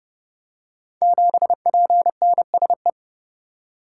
Morse code, heard as a keyed tone, about 30 wpm, 700 Hz. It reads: 7PNSE
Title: Code Groups - 5 Letter:Numbers - 30wpm